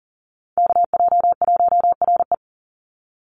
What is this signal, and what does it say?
Morse code, heard as a keyed tone, about 40 wpm, 700 Hz. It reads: KJ1RE
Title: Call Signs - 40wpm